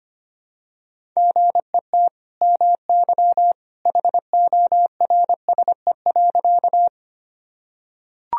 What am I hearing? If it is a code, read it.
Morse code, heard as a keyed tone, about 25 wpm, 700 Hz. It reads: GET MY HORSE.
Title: Sentences from Top 1000 Words - Review Part 1 - 25wpm